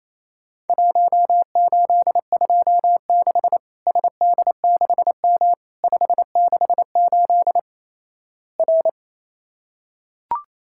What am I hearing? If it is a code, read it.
Morse code, heard as a keyed tone, about 28 wpm, 700 Hz. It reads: 1826 SD6M 568 R
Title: Worked All Europe DX Contest - QTC Exchange - 28wpm